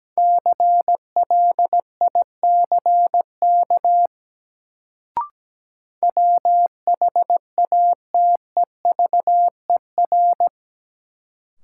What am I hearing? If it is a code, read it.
Morse code, heard as a keyed tone, about 17 wpm, 700 Hz. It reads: CLICK WHATEVER
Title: Sentences from Top 1100 Words - New Word Drill Alt - 17wpm